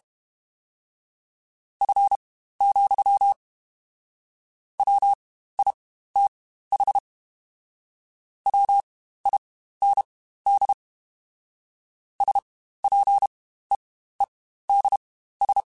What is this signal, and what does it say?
Morse code, heard as a keyed tone, about 32 wpm, spaced as for 15 wpm, 800 Hz. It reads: F, WITH WIND SPEEDS
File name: Spaced Characters / Spaced Characters-mp3s / Weather 4 14 sp @30